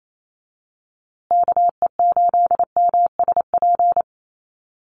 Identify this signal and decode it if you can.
Morse code, heard as a keyed tone, about 28 wpm, 700 Hz. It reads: KE8MSP
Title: USA Call Signs Repeated 2x 28wpm